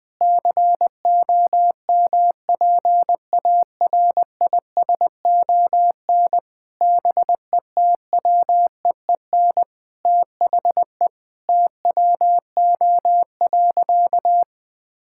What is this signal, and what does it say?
Morse code, heard as a keyed tone, about 20 wpm, 700 Hz. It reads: COMPARISON BETWEEN THE TWO.
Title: Sentences from Top 1600 Words 20wpm